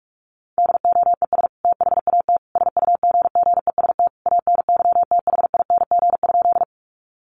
Morse code, heard as a keyed tone, about 44 wpm, 700 Hz. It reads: DOES THAT SUGGEST ANYTHING?
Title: Sentences from Top 1300 Words 45wpm